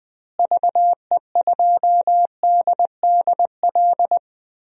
Morse code, heard as a keyed tone, about 20 wpm, 700 Hz. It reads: VE2DDL
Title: DXCC-100 Call Signs - 6 Characters - Repeated 2x - 20wpm